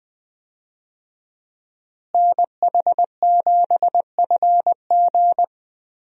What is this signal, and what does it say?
Morse code, heard as a keyed tone, about 20 wpm, 700 Hz. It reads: NH7FG
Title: USA Call Signs - 5 Characters - Repeated 2x - 20wpm